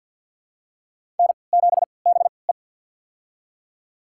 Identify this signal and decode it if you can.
Morse code, heard as a keyed tone, about 53 wpm, spaced as for 29 wpm, 700 Hz. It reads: N7BE
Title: USA Call Signs - 4 Characters - 3x Character Spacing - 50wpm